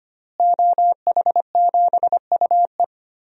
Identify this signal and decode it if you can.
Morse code, heard as a keyed tone, about 25 wpm, 700 Hz. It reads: OH7UE
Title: DXCC-100 Call Signs - 5 Characters - Repeated 3x - 25wpm